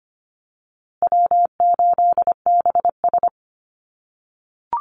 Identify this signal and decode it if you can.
Morse code, heard as a keyed tone, about 25 wpm, 700 Hz. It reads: W8BS
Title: USA Call Signs Repeated 2x 25wpm